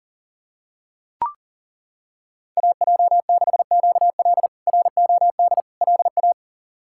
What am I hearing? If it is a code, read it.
Morse code, heard as a keyed tone, about 40 wpm, 700 Hz. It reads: AJ6QL ROD LA
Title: K1USN SST QSO Practice - 40wpm